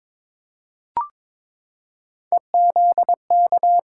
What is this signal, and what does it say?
Morse code, heard as a keyed tone, about 22 wpm, 700 Hz. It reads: EZK
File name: practice / 22wpm / Three Letter-Number Combinations - 22wpm